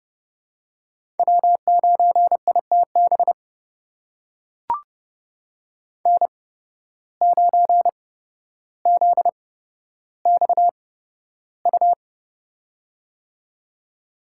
Morse code, heard as a keyed tone, about 32 wpm, spaced as for 13 wpm, 700 Hz. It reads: W9ITB N9ZXU
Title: USA Call Signs - 8x Character Spacing - 30wpm